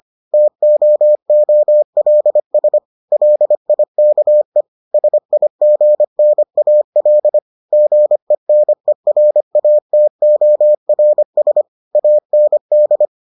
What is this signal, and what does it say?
Morse code, heard as a keyed tone, about 25 wpm, 600 Hz. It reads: TOOLS LIKE SIGNAL GENERATORS AND